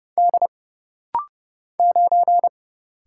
Morse code, heard as a keyed tone, about 30 wpm, 700 Hz. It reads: D 9
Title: Single Letter-Number - Rapid-Fire 30wpm